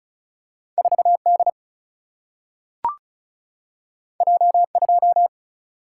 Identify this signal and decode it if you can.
Morse code, heard as a keyed tone, about 36 wpm, 700 Hz. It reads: 4 D J 2
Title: Single-Letter Single-Number 35wpm